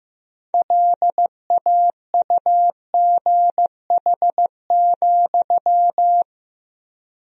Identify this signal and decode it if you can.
Morse code, heard as a keyed tone, about 15 wpm, 700 Hz. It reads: LAUGH,
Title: 4x Word-Spacing - Sentences from Top 500 Words - 15wpm